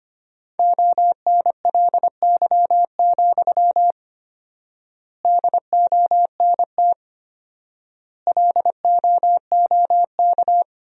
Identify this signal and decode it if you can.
Morse code, heard as a keyed tone, about 25 wpm, 700 Hz. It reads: ONLY, DONT LOOK
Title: 4x Word-Spacing - Sentences from Top 300 Words - 25wpm